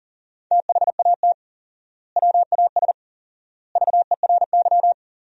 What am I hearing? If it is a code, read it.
Morse code, heard as a keyed tone, about 40 wpm, 700 Hz. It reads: THAT WAS VERY
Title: 4x Word-Spacing - Sentences from Top 200 Words - 40wpm